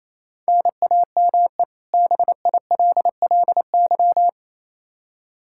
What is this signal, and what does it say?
Morse code, heard as a keyed tone, about 28 wpm, 700 Hz. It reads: NAME BILLY
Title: Practice QSO - Exchange 1 - Beginning - 28wpm